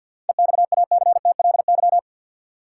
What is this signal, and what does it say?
Morse code, heard as a keyed tone, about 51 wpm, 700 Hz. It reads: EXACTLY
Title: Sentences from Top 900 Words - New Word Drill - 50wpm